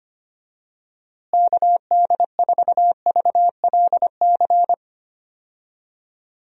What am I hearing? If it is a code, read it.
Morse code, heard as a keyed tone, about 25 wpm, 700 Hz. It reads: KD4VLC
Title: USA Call Signs - 6 Characters - Repeated 2x - 25wpm